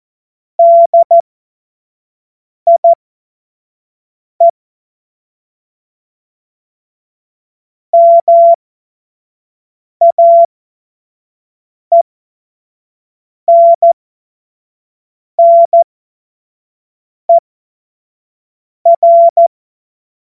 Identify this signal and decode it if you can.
Morse code, heard as a keyed tone, about 14 wpm, spaced as for 5 wpm, 680 Hz. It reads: DIE MAENNER
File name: Morsekurs / Witze / Witz 26. April 2022 25 F70